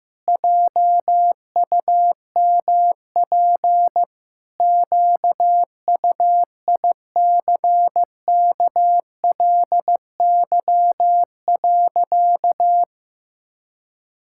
Morse code, heard as a keyed tone, about 15 wpm, 700 Hz. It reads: JUMP QUICKLY.
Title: Pangram Challenge v2 - 15wpm